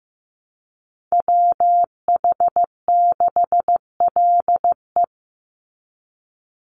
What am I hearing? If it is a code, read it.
Morse code, heard as a keyed tone, about 15 wpm, 700 Hz. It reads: WH6LE
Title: NA SOTA Chaser Callsigns 15wpm